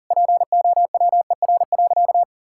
Morse code, heard as a keyed tone, about 37 wpm, 700 Hz. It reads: POWER.
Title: Sentences from Top 1000 Words - Review Part 5 - 40wpm